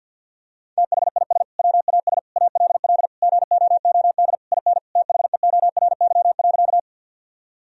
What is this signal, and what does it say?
Morse code, heard as a keyed tone, about 51 wpm, 700 Hz. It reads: THIS WAS ALL GOOD IN THEORY.
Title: Sentences from Top 1000 Words - Review Part 3 - 50wpm